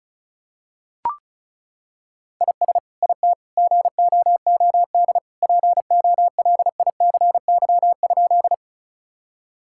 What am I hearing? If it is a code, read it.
Morse code, heard as a keyed tone, about 36 wpm, 690 Hz. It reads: IS IT GOOD POLICY?
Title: Sentences from Top 1000 Words - Review Part 5 - 35wpm